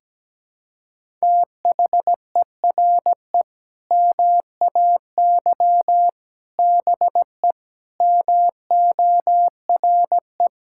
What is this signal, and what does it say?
Morse code, heard as a keyed tone, about 17 wpm, 700 Hz. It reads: THERE MAY BE MORE
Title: Sentences from Top 1000 Words - Review Part 6 - 17wpm